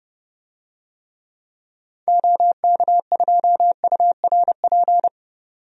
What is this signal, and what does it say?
Morse code, heard as a keyed tone, about 31 wpm, 700 Hz. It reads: OK2URP
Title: DXCC-100 Call Signs - 6 Characters - 30wpm